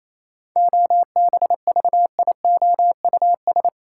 Morse code, heard as a keyed tone, about 28 wpm, 700 Hz. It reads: OBVIOUS
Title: Sentences from Top 1600 Words - New Word Drill - 28wpm